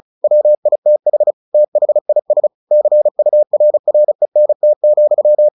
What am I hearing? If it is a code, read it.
Morse code, heard as a keyed tone, about 34 wpm, 600 Hz. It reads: WITH THIS CURRENT,
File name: the old man and the sea / the old man and the sea 1 35wpm